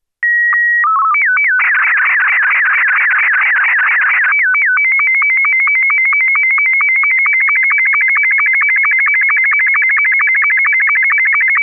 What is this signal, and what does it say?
SSTV (slow-scan television), Martin 2.
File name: sstv-essexham-image02-martin2